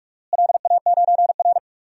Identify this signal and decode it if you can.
Morse code, heard as a keyed tone, about 46 wpm, 700 Hz. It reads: LA9R
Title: DXCC-100 Call Signs - 4 Characters - Repeated 2x - 45wpm